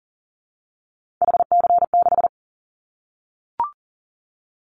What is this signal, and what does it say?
Morse code, heard as a keyed tone, about 40 wpm, 700 Hz. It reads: HC6
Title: Callsign Prefixes for Top 100 DXCC - 40wpm